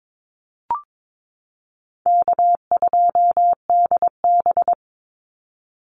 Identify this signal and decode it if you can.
Morse code, heard as a keyed tone, about 22 wpm, 700 Hz. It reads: K2DB
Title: Easy Call Signs Repeated 3x - 22wpm